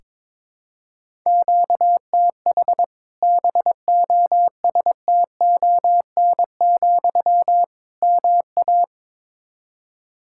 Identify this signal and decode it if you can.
Morse code, heard as a keyed tone, about 22 wpm, 700 Hz. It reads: QTH BOSTON, MA
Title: Practice QSO - Exchange 1 - Beginning - 22wpm